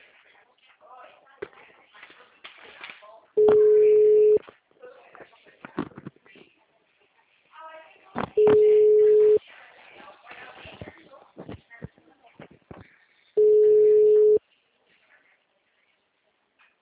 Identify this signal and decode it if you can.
Ringback tone (Europe, 425 Hz), with speech.